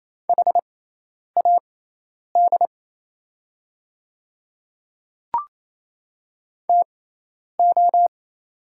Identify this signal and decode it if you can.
Morse code, heard as a keyed tone, about 28 wpm, spaced as for 10 wpm, 690 Hz. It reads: HAD TO
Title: 6x Character Spacing - Top 100 Words - 28wpm